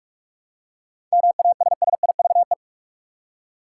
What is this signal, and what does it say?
Morse code, heard as a keyed tone, about 47 wpm, 700 Hz. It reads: MASSIVE
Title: Seven Letter Words - 45wpm